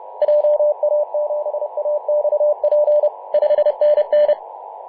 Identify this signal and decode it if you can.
Morse code, heard as a keyed tone, about 30 wpm, 600 Hz, with radio noise. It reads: JA7AXP 5NN
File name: Mar/20 2349z FW5JJ 28.011MHz CW